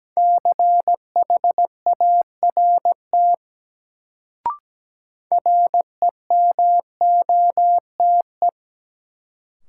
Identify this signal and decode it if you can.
Morse code, heard as a keyed tone, about 17 wpm, 700 Hz. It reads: CHART REMOTE
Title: Sentences from Top 1300 Words - New Word Drill Alt - 17wpm